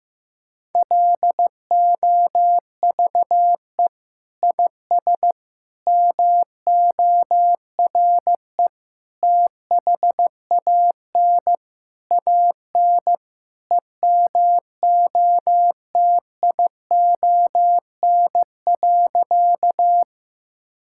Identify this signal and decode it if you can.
Morse code, heard as a keyed tone, about 15 wpm, 700 Hz. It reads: LOVE IS MORE THAN AN EMOTION.
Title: Sentences from Top 1600 Words 15wpm